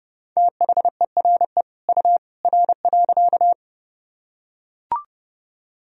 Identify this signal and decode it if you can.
Morse code, heard as a keyed tone, about 30 wpm, 700 Hz. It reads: THERE U R.
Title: Sentences from Top 100 Words and Abbreviations 30wpm